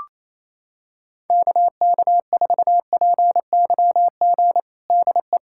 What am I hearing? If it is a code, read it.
Morse code, heard as a keyed tone, about 28 wpm, 700 Hz. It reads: KK4PYG DE
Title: USA Callsign DE Callsign 28wpm